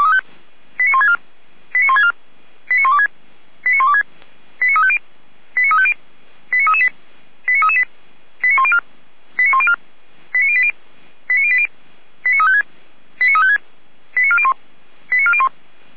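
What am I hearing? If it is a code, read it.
Five-tone selective call (ZVEI), heard as quick runs of five tones, tones of about 70 ms. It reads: 89164 89164 89126 89126 89350 89350 89338 89338 89104 89104 89080 89080 89356 89356 89401 89401
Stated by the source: ZVEI five-tone selective call received on SDR.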